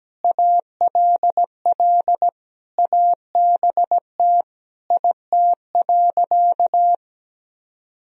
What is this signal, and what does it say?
Morse code, heard as a keyed tone, about 17 wpm, 700 Hz. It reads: ALL ABT IT.
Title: Sentences from Top 200 Words and Abbreviations 17wpm